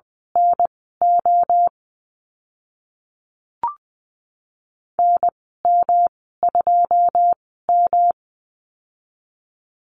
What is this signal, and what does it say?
Morse code, heard as a keyed tone, about 20 wpm, spaced as for 14 wpm, 700 Hz. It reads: NO NM2M
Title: 128 Lesson 21g - Number 6 - Call Signs 20wpm